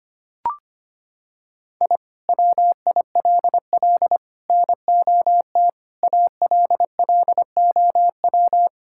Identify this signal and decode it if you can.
Morse code, heard as a keyed tone, about 25 wpm, 700 Hz. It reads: I WILL NOT ALLOW
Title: Sentences from Top 1000 Words - Review Part 5 - 25wpm